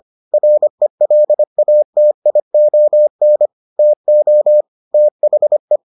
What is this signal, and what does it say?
Morse code, heard as a keyed tone, about 25 wpm, 600 Hz. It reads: RELATION TO THE